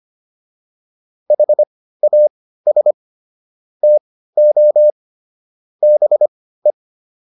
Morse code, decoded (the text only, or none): HAS TO BE